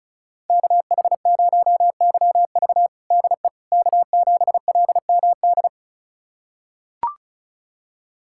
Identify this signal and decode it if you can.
Morse code, heard as a keyed tone, about 36 wpm, 700 Hz. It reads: KH0YV DE K7LMD
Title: USA Callsign DE Callsign - 5 Characters - 35wpm